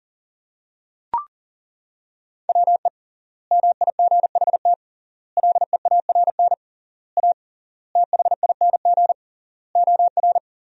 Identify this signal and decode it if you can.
Morse code, heard as a keyed tone, about 40 wpm, 700 Hz. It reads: WE MIGHT LEARN A THING OR
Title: 3x Word-Spacing - Sentences from Top 300 Words - 40wpm